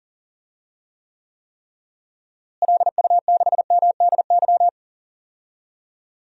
Morse code, heard as a keyed tone, about 40 wpm, 700 Hz. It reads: LU6MDY
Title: DXCC-100 Call Signs - 6 Characters - Repeated 3x - 40wpm